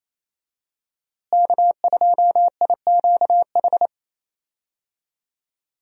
Morse code, heard as a keyed tone, about 28 wpm, 700 Hz. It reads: K2IQH
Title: USA Call Signs - 5 Characters - Repeated 3x - 28wpm